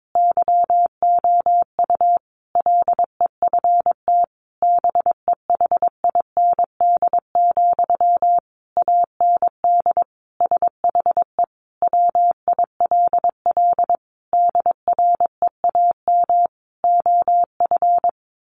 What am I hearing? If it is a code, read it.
Morse code, heard as a keyed tone, about 22 wpm, 700 Hz. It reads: YOU LEFT BEHIND, AND SHE WILL DREAM OF
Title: Sentences from Top 1500 Words 22wpm